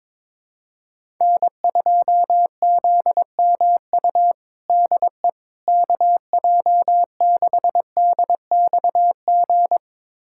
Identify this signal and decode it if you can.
Morse code, heard as a keyed tone, about 21 wpm, 700 Hz. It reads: N2ZMU DE KJ6DXG
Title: USA Callsign DE Callsign 22wpm